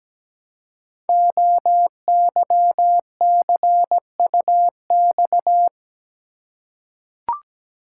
Morse code, heard as a keyed tone, about 17 wpm, 700 Hz. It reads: OYCUX
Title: Code Groups - 5 Letter:Numbers - 17wpm